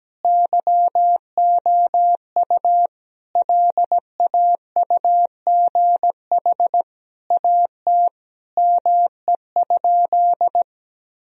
Morse code, heard as a keyed tone, about 17 wpm, 700 Hz. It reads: YOU LAUGH AT ME?
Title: Sentences from Top 1000 Words - Review Part 2 - 17wpm